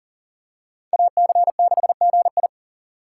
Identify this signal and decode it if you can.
Morse code, heard as a keyed tone, about 40 wpm, 700 Hz. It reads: AC6GI
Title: USA Call Signs - 5 Characters - 6x Character Spacing - 40wpm